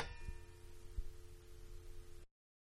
Dial tone (North America), 350 + 440 Hz, with noise.
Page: Telephone-Pick Up 2